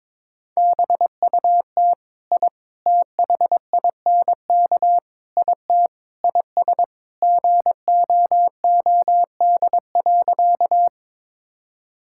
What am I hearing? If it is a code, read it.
Morse code, heard as a keyed tone, about 22 wpm, 700 Hz. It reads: BUT I THINK IT IS GOOD.
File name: practice / 22wpm / Sentences with 4-Word Phrases - 22wpm